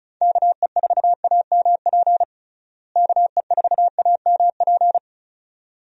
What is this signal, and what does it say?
Morse code, heard as a keyed tone, about 36 wpm, 700 Hz. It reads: KE4AMP KE4AMP
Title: USA Call Signs - 6 Characters - Repeated 2x - 35wpm